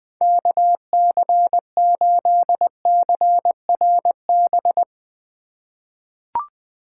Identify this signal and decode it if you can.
Morse code, heard as a keyed tone, about 20 wpm, 700 Hz. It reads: KC8CRB
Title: USA Call Signs - 6 Characters - 4x Character Spacing - 20wpm